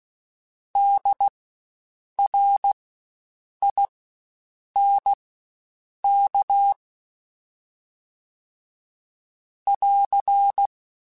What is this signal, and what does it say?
Morse code, heard as a keyed tone, about 16 wpm, spaced as for 7 wpm, 800 Hz. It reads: DRINK <AR>
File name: Spaced Alphas / Spaced Alphas mp3s / Ds sp @15